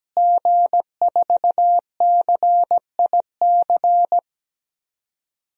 Morse code, heard as a keyed tone, about 17 wpm, 700 Hz. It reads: G4CIC
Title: DXCC-100 Call Signs - 5 Characters - Repeated 3x - 17wpm